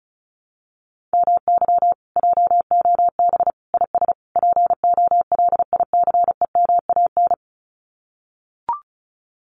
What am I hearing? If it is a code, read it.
Morse code, heard as a keyed tone, about 34 wpm, 700 Hz. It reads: MY JOB IS POLICEMAN
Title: Common QSO Phrases Encore 35wpm